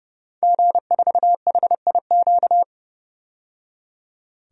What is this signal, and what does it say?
Morse code, heard as a keyed tone, about 30 wpm, 700 Hz. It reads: G4HIQ
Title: DXCC-100 Call Signs - 5 Characters - 30wpm